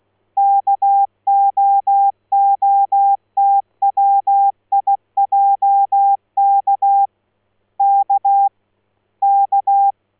Morse code, heard as a keyed tone, about 16 wpm, 800 Hz. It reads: KOOTWIJK K K
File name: Morse-Hier-Radio-Kootwijk